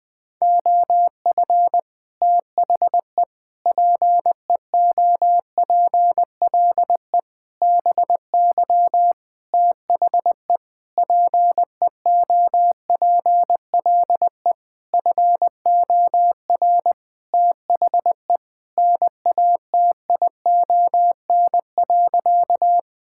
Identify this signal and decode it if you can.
Morse code, heard as a keyed tone, about 20 wpm, 700 Hz. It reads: OF THE PEOPLE BY THE PEOPLE FOR THE NATION.
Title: Sentences with 8-Word Phrases - 20wpm